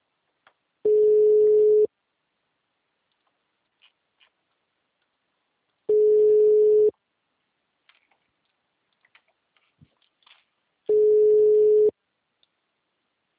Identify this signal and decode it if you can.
Ringback tone (Europe, 425 Hz).